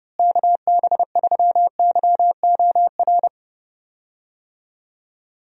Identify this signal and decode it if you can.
Morse code, heard as a keyed tone, about 32 wpm, 700 Hz. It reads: KB3YOR
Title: USA Call Signs - 6 Characters - Repeated 2x - 30wpm